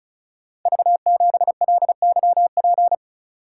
Morse code, heard as a keyed tone, about 36 wpm, 700 Hz. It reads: V7LYP
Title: Code Groups - 5 Letter:Numbers - 35wpm